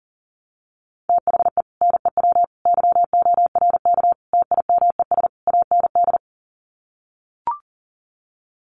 Morse code, heard as a keyed tone, about 40 wpm, 700 Hz. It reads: THE NEW YORK TIMES AND
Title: Sets of 5 Words Encore 40wpm